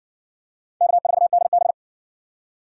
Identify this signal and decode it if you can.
Morse code, heard as a keyed tone, about 59 wpm, 700 Hz. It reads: K4DB